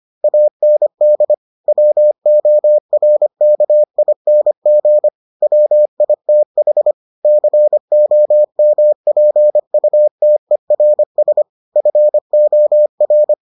Morse code, heard as a keyed tone, about 25 wpm, 600 Hz. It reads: AND WORKING WITH COMPUTERS FOR